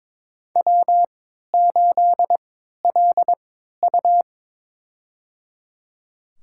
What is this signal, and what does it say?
Morse code, heard as a keyed tone, about 22 wpm, spaced as for 13 wpm, 700 Hz. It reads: W8LU
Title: USA Call Signs - 4 Characters - 3x Character Spacing - 22wpm